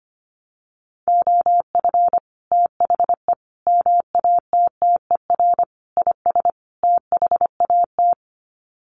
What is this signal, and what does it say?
Morse code, heard as a keyed tone, about 25 wpm, 700 Hz. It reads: OF THE MATTER IS THAT
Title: Sets of 5 Words Encore 25wpm